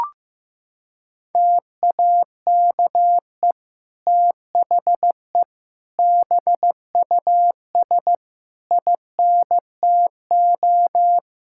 Morse code, heard as a keyed tone, about 15 wpm, 700 Hz. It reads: TAKE THE BUS INTO
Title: Sentences from Top 1000 Words - Review Part 6 - 15wpm